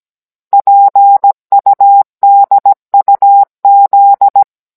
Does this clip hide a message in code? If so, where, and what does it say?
Yes: Morse code from 0.5 to 4.4 s, heard as a keyed tone, about 17 wpm, 800 Hz. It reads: PUDUZ